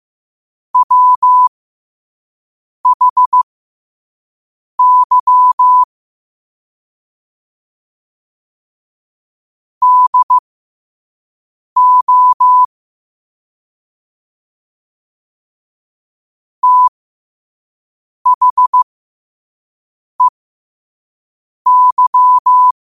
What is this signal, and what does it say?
Morse code, heard as a keyed tone, about 15 wpm, spaced as for 5 wpm, 1000 Hz. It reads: WHY DO THEY
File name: qotdfiles / 5wpm